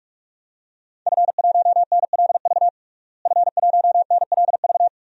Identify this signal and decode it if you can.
Morse code, heard as a keyed tone, about 42 wpm, 700 Hz. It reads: F1NLV F1NLV
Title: DXCC-100 Call Signs - 5 Characters - Repeated 3x - 45wpm